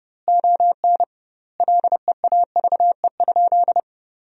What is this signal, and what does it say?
Morse code, heard as a keyed tone, about 32 wpm, 700 Hz. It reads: ON LEAVE?
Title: 2x Word-Spacing - Sentences from Top 400 Words - 30wpm